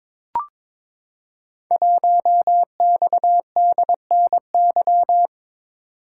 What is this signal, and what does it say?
Morse code, heard as a keyed tone, about 22 wpm, 700 Hz. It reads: 1XDNY
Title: Code Groups - 5 Letter:Numbers - 22wpm